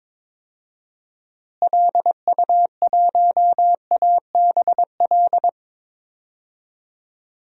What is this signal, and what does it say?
Morse code, heard as a keyed tone, about 22 wpm, 700 Hz. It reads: LU1ABL
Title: DXCC-100 Call Signs - 6 Characters - Repeated 3x - 22wpm